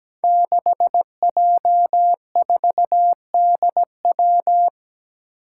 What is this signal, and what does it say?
Morse code, heard as a keyed tone, about 17 wpm, 700 Hz. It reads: 6J4DW
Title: Five Challenging Letter-Number Combinations 17wpm